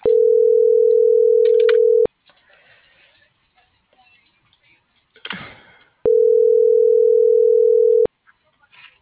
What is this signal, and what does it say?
Ringback tone (North America), 440 + 480 Hz.